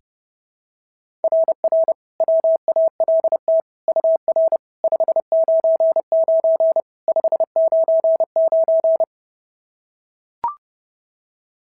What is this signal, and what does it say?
Morse code, heard as a keyed tone, about 30 wpm, 690 Hz. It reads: RR WALT UR 599 599
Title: SOTA QSO Practice 30wpm